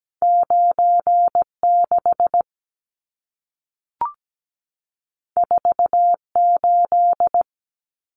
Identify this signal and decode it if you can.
Morse code, heard as a keyed tone, about 17 wpm, 700 Hz. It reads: 96 48
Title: Numbers - 2-Digit 17wpm